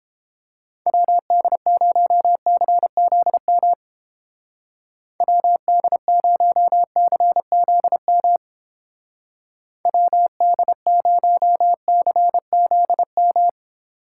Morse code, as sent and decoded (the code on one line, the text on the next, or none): .-- -.. ----- -.-. --.. -- / .-- -.. ----- -.-. --.. -- / .-- -.. ----- -.-. --.. --
WD0CZM WD0CZM WD0CZM